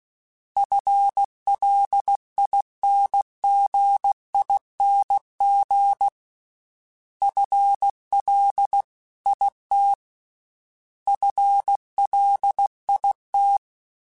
Morse code, heard as a keyed tone, about 16 wpm, 800 Hz. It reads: FLINGING FLIT FLIT
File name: FL / FL mp3s / FL Prefixes 3x2 sp @15